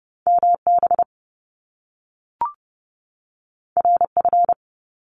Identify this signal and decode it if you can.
Morse code, heard as a keyed tone, about 30 wpm, 700 Hz. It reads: MB RF
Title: Top Two Letter Combinations in Words 30wpm